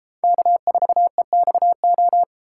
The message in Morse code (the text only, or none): K4EXO